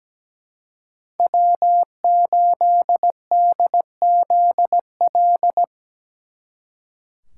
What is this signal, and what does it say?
Morse code, heard as a keyed tone, about 19 wpm, 700 Hz. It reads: W8DZL
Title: Speed Racing - USA 1x3 Call Signs 17wpm